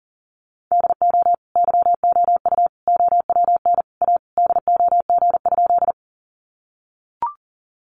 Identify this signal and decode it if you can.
Morse code, heard as a keyed tone, about 40 wpm, 700 Hz. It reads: DO YOU OWN A DOG?
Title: Sentences from Top 400 Words 40wpm